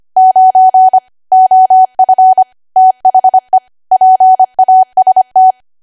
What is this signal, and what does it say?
Morse code, heard as a keyed tone, about 25 wpm, 750 Hz. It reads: 9 OF THE PAST
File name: cwa / W1AW-Level3 / 25WPM